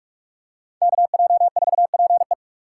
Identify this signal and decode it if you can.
Morse code, heard as a keyed tone, about 43 wpm, 700 Hz. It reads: KJ4PE
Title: USA Call Signs - 3x Character Spacing - 45wpm